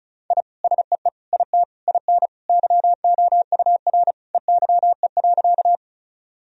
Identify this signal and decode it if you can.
Morse code, heard as a keyed tone, about 36 wpm, 700 Hz. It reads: I SEE IT IN YOUR EYE.
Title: Sentences from Top 1000 Words - Review Part 1 - 35wpm